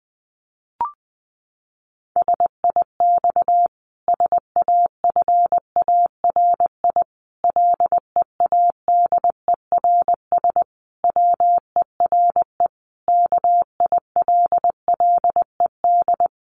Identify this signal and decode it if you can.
Morse code, heard as a keyed tone, about 20 wpm, 700 Hz. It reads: SIX SAFARI LEADERS WERE KILLED
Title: A Sound Of Thunder 20wpm